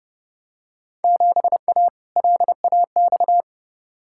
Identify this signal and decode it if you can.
Morse code, heard as a keyed tone, about 30 wpm, 700 Hz. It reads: 7A LAX
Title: Field Day Exchanges Encore - 30wpm